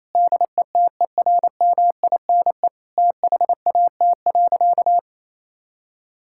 Morse code, heard as a keyed tone, about 28 wpm, 700 Hz. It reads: DETERMINE THAT.
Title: Sentences from Top 1000 Words - Review Part 5 - 28wpm